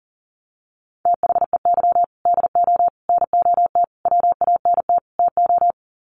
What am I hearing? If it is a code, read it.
Morse code, heard as a keyed tone, about 37 wpm, 700 Hz. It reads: THEY DO NOT WANT TO
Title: Sets of 5 Words 40wpm